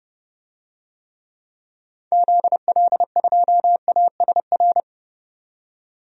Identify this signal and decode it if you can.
Morse code, heard as a keyed tone, about 30 wpm, 700 Hz. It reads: ZL2ASR
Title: DXCC-100 Call Signs - 6 Characters - 30wpm